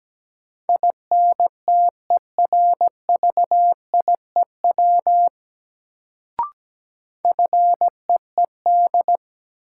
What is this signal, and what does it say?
Morse code, heard as a keyed tone, about 17 wpm, 700 Hz. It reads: INTERVIEW FEED
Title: Sentences from Top 1600 Words - New Word Drill Alt - 17wpm